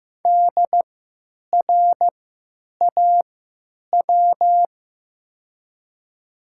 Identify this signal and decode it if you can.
Morse code, heard as a keyed tone, about 15 wpm, spaced as for 9 wpm, 700 Hz. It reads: DRAW
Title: 3x Character Spacing - Top 300 Words - 15wpm